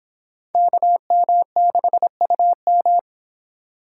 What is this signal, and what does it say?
Morse code, heard as a keyed tone, about 29 wpm, 700 Hz. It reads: KM6UM